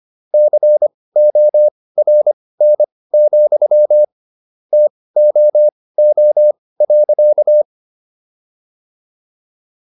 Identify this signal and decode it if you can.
Morse code, heard as a keyed tone, about 25 wpm, spaced as for 18 wpm, 600 Hz. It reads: CORN, TOO.